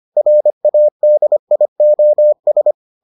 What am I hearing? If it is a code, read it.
Morse code, heard as a keyed tone, about 25 wpm, 600 Hz. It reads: RADIOS